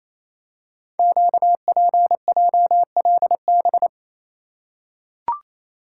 Morse code, heard as a keyed tone, about 28 wpm, 700 Hz. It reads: QPJLB
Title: Code Groups - 5 Letters - 28wpm